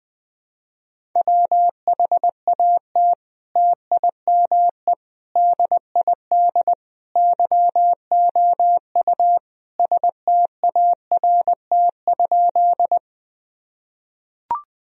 Morse code, heard as a keyed tone, about 21 wpm, 700 Hz. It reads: WHAT TIME DID YOU START?
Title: Sentences from Top 1000 Words - Review Part 1 - 20wpm